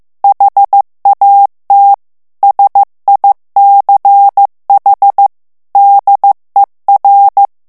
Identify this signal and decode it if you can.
Morse code, heard as a keyed tone, about 15 wpm, 800 Hz. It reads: HAT SICH DER
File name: ALLEMAND texte clair 1 vitesse 14 mots minute